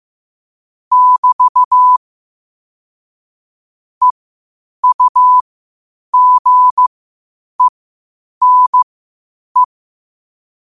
Morse code, heard as a keyed tone, about 15 wpm, spaced as for 8 wpm, 1000 Hz. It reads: <BT> EUGENE